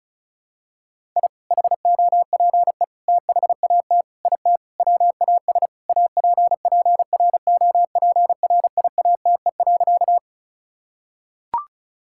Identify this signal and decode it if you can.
Morse code, heard as a keyed tone, about 36 wpm, 700 Hz. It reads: I HOPE THAT IT WAS APPROPRIATE.
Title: Sentences from Top 1000 Words - Review Part 5 - 35wpm